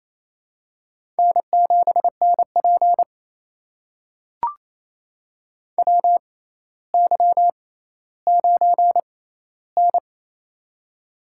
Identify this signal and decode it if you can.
Morse code, heard as a keyed tone, about 28 wpm, spaced as for 14 wpm, 700 Hz. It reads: N7NP WY9N
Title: USA Call Signs - 4 Characters - 6x Character Spacing - 28wpm